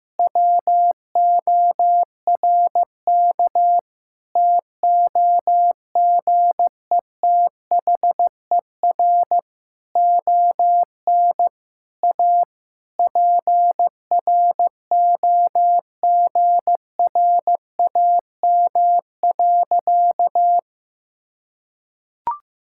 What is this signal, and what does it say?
Morse code, heard as a keyed tone, about 15 wpm, 700 Hz. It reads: WORK TOGETHER ON A PROGRAM.
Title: Sentences from Top 1000 Words - Review Part 5 - 15wpm